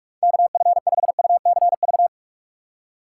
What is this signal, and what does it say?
Morse code, heard as a keyed tone, about 46 wpm, 700 Hz. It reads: KF5UCV
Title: Speed Racing - USA 2x3 Call Signs 45wpm